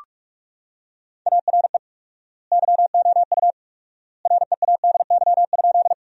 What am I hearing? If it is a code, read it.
Morse code, heard as a keyed tone, about 47 wpm, 700 Hz. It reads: ARE YOU READY?
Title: 4x Word-Spacing - Sentences from Top 400 Words - 45wpm